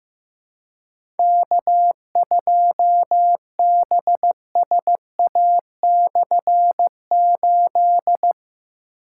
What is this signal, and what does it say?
Morse code, heard as a keyed tone, about 16 wpm, 700 Hz. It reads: K2BSA/8
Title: K1USN SST QSO Practice - 15wpm